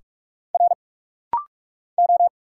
Morse code, heard as a keyed tone, about 46 wpm, 700 Hz. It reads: R O
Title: Single Letters - Rapid-Fire 45wpm